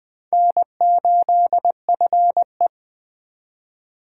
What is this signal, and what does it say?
Morse code, heard as a keyed tone, about 23 wpm, 700 Hz. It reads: N8FE
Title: Speed Racing - USA 1x2 Call Signs 20wpm